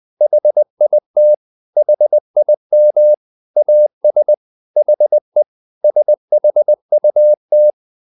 Morse code, heard as a keyed tone, about 20 wpm, 600 Hz. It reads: HIT HIM AS HE SHUT
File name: the old man and the sea part 2 20wpm